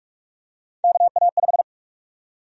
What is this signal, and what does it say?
Morse code, heard as a keyed tone, about 48 wpm, 700 Hz. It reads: KA5
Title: USA Call Sign Prefixes - Repeated 2x - 45wpm